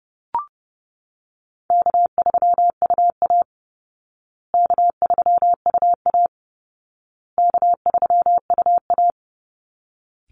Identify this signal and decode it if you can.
Morse code, heard as a keyed tone, about 30 wpm, 700 Hz. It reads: K3UA K3UA K3UA
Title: Call Signs Repeated 2x - 30wpm